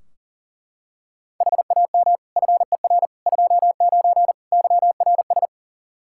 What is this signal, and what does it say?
Morse code, heard as a keyed tone, about 40 wpm, 700 Hz. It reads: HAM FER 29 YRS
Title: Practice QSO - Exchange 3 - More Info - 40wpm